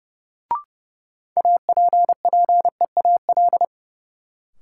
Morse code, heard as a keyed tone, about 30 wpm, 700 Hz. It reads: APPEAL
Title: Sentences from Top 1300 Words - New Word Drill Alt - 30wpm